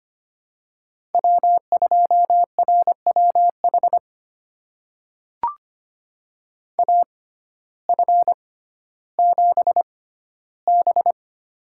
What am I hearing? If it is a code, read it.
Morse code, heard as a keyed tone, about 25 wpm, spaced as for 14 wpm, 700 Hz. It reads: W2RWH AF7B
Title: USA Call Signs - 6x Character Spacing - 25wpm